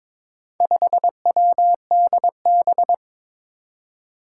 Morse code, heard as a keyed tone, about 21 wpm, 700 Hz. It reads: 5WDB
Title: Four Challenging Letter-Number Combinations 22wpm